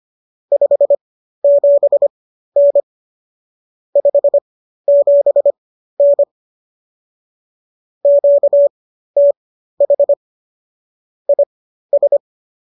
Morse code, heard as a keyed tone, about 25 wpm, spaced as for 13 wpm, 570 Hz. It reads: 57N 57N QTH IS
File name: qso213 13